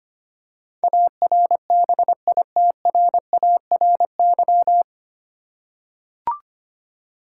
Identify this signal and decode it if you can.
Morse code, heard as a keyed tone, about 25 wpm, 700 Hz. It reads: ARBITRARY
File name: practice / 25wpm / 9 Letter Words - First 3 Letters Given - 25wpm